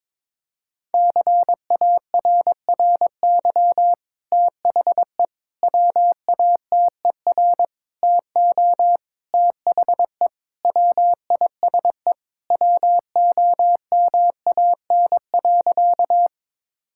Morse code, heard as a keyed tone, about 22 wpm, 700 Hz. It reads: CARRY THE WATER TO THE WISE WOMAN.
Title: Sentences from Top 1600 Words 22wpm